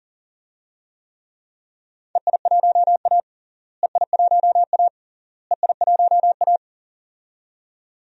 Morse code, heard as a keyed tone, about 40 wpm, 700 Hz. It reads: EI1A EI1A EI1A
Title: DXCC-100 Call Signs - 4 Characters - Repeated 3x - 40wpm